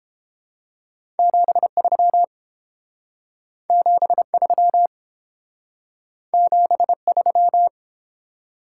Morse code, heard as a keyed tone, about 28 wpm, 690 Hz. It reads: 73 73 73